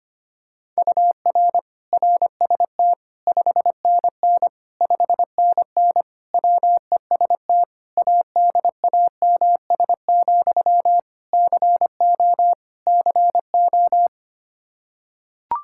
Morse code, heard as a keyed tone, about 25 wpm, 700 Hz. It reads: UR RST 5NN 5NN WEST ADAMS, CO CO
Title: K1USN SST QSO Practice 25wpm